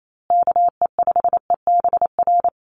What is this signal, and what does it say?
Morse code, heard as a keyed tone, about 28 wpm, 700 Hz. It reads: KE5EBR
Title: USA Call Signs Repeated 2x 28wpm